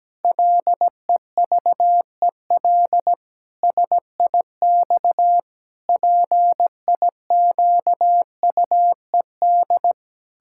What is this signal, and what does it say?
Morse code, heard as a keyed tone, about 17 wpm, 700 Hz. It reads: LEVEL SIX PIQUED
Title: Pangram Challenge v2 - 17wpm